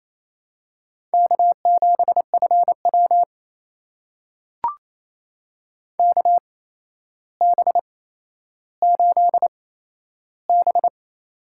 Morse code, heard as a keyed tone, about 28 wpm, spaced as for 12 wpm, 700 Hz. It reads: K7FW KB8B
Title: USA Call Signs - 4 Characters - 8x Character Spacing - 28wpm